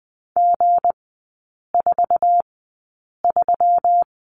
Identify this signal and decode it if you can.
Morse code, heard as a keyed tone, about 20 wpm, spaced as for 8 wpm, 700 Hz. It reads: G43
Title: 159 Lesson 26e - Letter g - Three Characters 20wpm